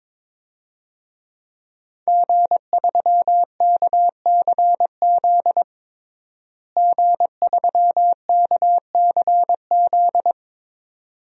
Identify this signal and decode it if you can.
Morse code, heard as a keyed tone, about 22 wpm, 700 Hz. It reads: G3KCZ G3KCZ
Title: DXCC-100 Call Signs - 5 Characters - Repeated 2x - 22wpm